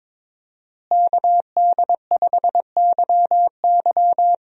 Morse code, heard as a keyed tone, about 22 wpm, 700 Hz. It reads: KD5YY
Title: USA Call Signs - 5 Characters - Repeated 2x - 22wpm